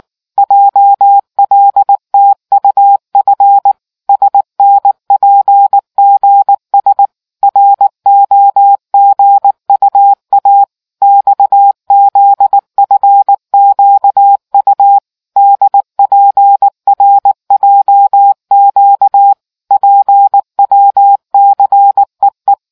Morse code, heard as a keyed tone, about 19 wpm, 800 Hz. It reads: JLTUF SNPGS ROGUA XZFQU DPRJQ PWCEE